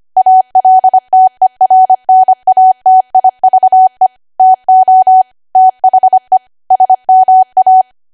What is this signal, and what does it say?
Morse code, heard as a keyed tone, about 25 wpm, 750 Hz. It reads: ALTERNATIVE TO THE SMA
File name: cwa / W1AW-Level3 / 25WPM